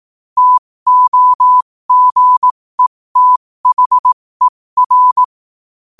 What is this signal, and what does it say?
Morse code, heard as a keyed tone, about 18 wpm, spaced as for 15 wpm, 1000 Hz. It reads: TOGETHER
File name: qotdfiles / 15wpm